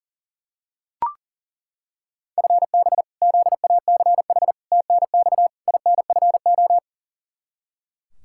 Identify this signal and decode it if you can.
Morse code, heard as a keyed tone, about 38 wpm, 700 Hz. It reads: FB ZACH TNX INFO
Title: K1USN SST QSO Practice 40wpm